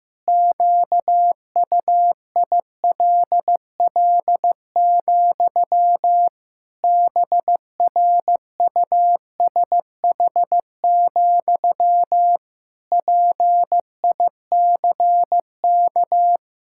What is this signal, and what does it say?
Morse code, heard as a keyed tone, about 15 wpm, 700 Hz. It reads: QUILL, BRUSH, PICK
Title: Pangram Challenge v2 - 15wpm